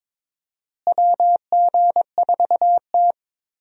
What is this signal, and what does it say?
Morse code, heard as a keyed tone, about 21 wpm, 700 Hz. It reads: WG4T
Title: USA Call Signs - 4 Characters - Repeated 2x - 22wpm